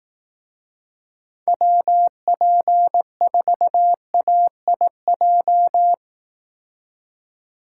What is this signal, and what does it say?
Morse code, heard as a keyed tone, about 18 wpm, 700 Hz. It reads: WP4AIJ
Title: Speed Racing - USA 2x3 Call Signs 15wpm